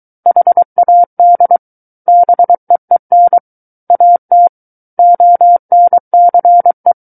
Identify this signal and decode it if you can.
Morse code, heard as a keyed tone, about 23 wpm, 700 Hz. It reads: HAD BEEN AT ONCE
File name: Book One / 14-Chapter 14 - 23 WPM